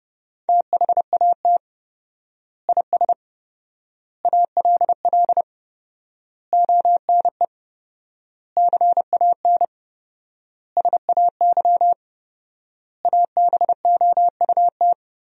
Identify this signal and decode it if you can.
Morse code, heard as a keyed tone, about 31 wpm, 700 Hz. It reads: THAT IS ALL ONE CAN SAY ABOUT
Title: 4x Word-Spacing - Sentences from Top 200 Words - 30wpm